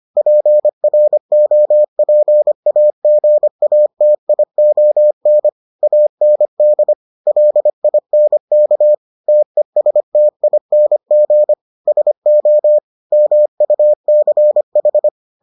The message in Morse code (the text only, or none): PROPAGATION AND LINK TESTING SO MUCH